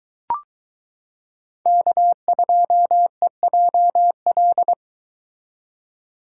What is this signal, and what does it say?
Morse code, heard as a keyed tone, about 22 wpm, 700 Hz. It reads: K2EJL